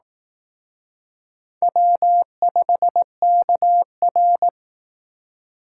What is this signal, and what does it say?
Morse code, heard as a keyed tone, about 19 wpm, 700 Hz. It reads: W5KR